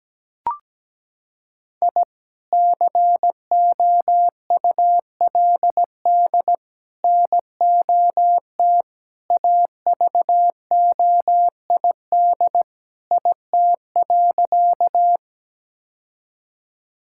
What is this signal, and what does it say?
Morse code, heard as a keyed tone, about 17 wpm, 700 Hz. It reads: I COULD NOT AVOID IT.
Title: Sentences from Top 1000 Words - Review Part 6 - 17wpm